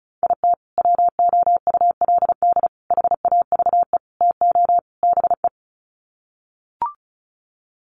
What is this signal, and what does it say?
Morse code, heard as a keyed tone, about 34 wpm, 700 Hz. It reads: IT WOULD HAVE TO BE
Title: Sets of 5 Words Encore 35wpm